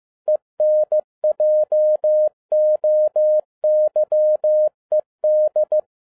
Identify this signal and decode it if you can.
Morse code, heard as a keyed tone, about 15 wpm, 610 Hz. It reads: ENJOYED
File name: 2011-07-mp3 / n47 JOHN K3TN 15wpm